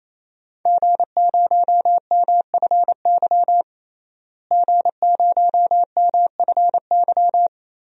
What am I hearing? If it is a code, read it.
Morse code, heard as a keyed tone, about 28 wpm, 700 Hz. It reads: G0MFY G0MFY
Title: DXCC-100 Call Signs - 5 Characters - Repeated 3x - 28wpm